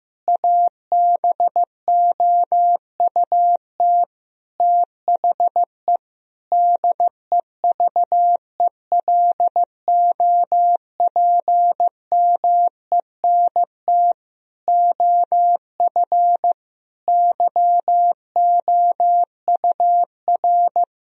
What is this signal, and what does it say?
Morse code, heard as a keyed tone, about 16 wpm, 700 Hz. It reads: ABOUT THE DEVELOPMENT OF YOUR
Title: Sentences from Top 1000 Words - Review Part 4 - 15wpm